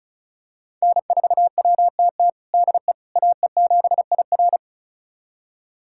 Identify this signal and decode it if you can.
Morse code, heard as a keyed tone, about 34 wpm, 700 Hz. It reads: N4WTT DE AE7IR
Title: USA Callsign DE Callsign - 5 Characters - 35wpm